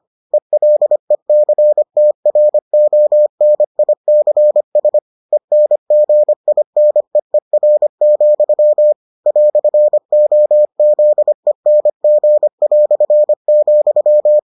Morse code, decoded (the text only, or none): ELECTRONICS ENGINEER, "OZENG",